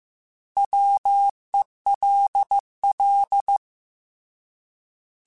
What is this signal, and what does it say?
Morse code, heard as a keyed tone, about 15 wpm, 800 Hz. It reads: WELL
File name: Alphabetical / Alphabetical mp3s / Ws @13